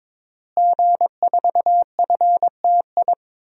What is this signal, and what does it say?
Morse code, heard as a keyed tone, about 22 wpm, 700 Hz. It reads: G4FTI
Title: DXCC-100 Call Signs - 5 Characters - Repeated 3x - 22wpm